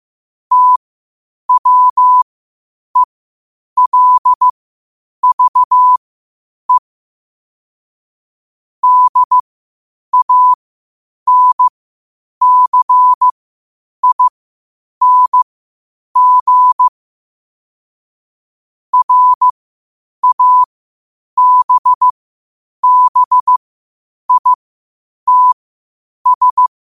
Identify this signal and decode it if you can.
Morse code, heard as a keyed tone, about 15 wpm, spaced as for 8 wpm, 1000 Hz. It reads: TWELVE DANCING RABBITS
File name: 8wpm